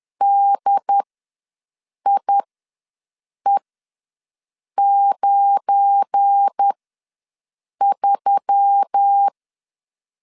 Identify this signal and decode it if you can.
Morse code, heard as a keyed tone, about 10 wpm, spaced as for 6 wpm, 800 Hz. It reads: DIE93
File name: LEZIONI / 8 YJ45 / 8 YJ4 ws 10wpm 4020